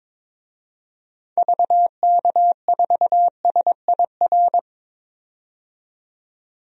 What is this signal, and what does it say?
Morse code, heard as a keyed tone, about 22 wpm, 700 Hz. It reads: VK4SIR
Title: DXCC-100 Call Signs - 6 Characters - 22wpm